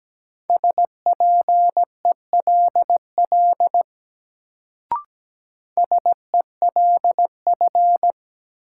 Morse code, heard as a keyed tone, about 17 wpm, 700 Hz. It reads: SPELL SELF
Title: Sentences from Top 1000 Words - Review Part 1 - New Word Drill - 17wpm